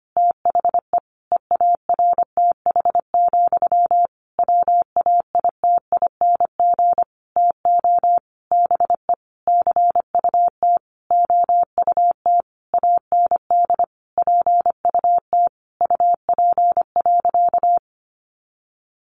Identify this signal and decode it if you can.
Morse code, heard as a keyed tone, about 25 wpm, 700 Hz. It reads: THE EARTH, WAITING TO BE CUT OUT AND PUT UP.
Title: A Sound Of Thunder 25wpm